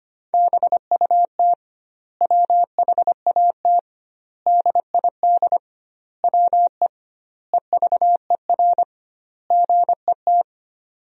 Morse code, heard as a keyed tone, about 25 wpm, 700 Hz. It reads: BUT WHAT DID WE EVER GET
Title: 2x Word-Spacing - Sentences from Top 400 Words - 25wpm